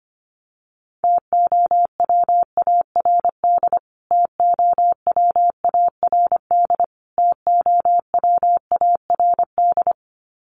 Morse code, heard as a keyed tone, about 25 wpm, 700 Hz. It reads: TOWARD TOWARD TOWARD
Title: Top 500 Words Repeated 25wpm